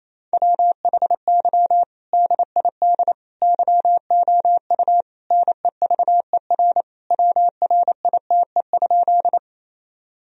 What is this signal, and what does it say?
Morse code, heard as a keyed tone, about 28 wpm, 700 Hz. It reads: WHY DID YOU NEVER WRITE?
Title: Sentences from Top 1000 Words - Review Part 1 - 28wpm